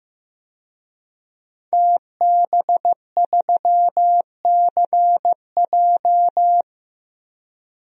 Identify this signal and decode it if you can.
Morse code, heard as a keyed tone, about 15 wpm, 700 Hz. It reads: TB3CJ
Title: DXCC-100 Call Signs - 5 Characters - Repeated 2x - 15wpm